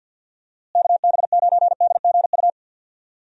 Morse code, heard as a keyed tone, about 51 wpm, 700 Hz. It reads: KB9DGU
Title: USA Call Signs - 6 Characters - Repeated 3x - 50wpm